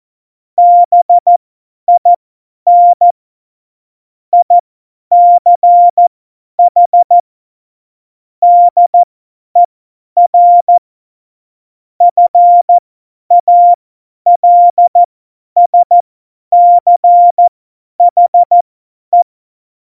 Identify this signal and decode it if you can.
Morse code, heard as a keyed tone, about 14 wpm, spaced as for 10 wpm, 700 Hz. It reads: BIN ICH DER FALSCHE